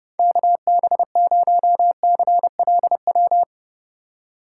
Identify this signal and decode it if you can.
Morse code, heard as a keyed tone, about 32 wpm, 700 Hz. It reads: KB0CLW
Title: USA Call Signs - 3x Character Spacing - 30wpm